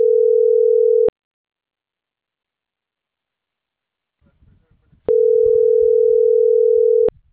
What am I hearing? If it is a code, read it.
Ringback tone (North America), 440 + 480 Hz.